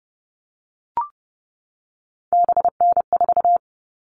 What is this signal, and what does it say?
Morse code, heard as a keyed tone, about 30 wpm, 700 Hz. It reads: BN4
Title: All Cut Numbers - 3-Digit 30wpm